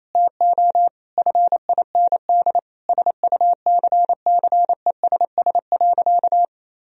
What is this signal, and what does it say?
Morse code, heard as a keyed tone, about 28 wpm, 700 Hz. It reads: TO FIND SUCCESS.
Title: Sentences from Top 1000 Words - Review Part 4 - 28wpm